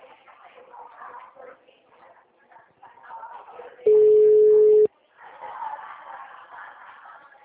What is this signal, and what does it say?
Ringback tone (Europe, 425 Hz).